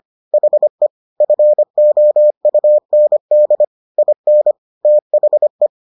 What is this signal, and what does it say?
Morse code, heard as a keyed tone, about 25 wpm, 600 Hz. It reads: HE FOUND IN THE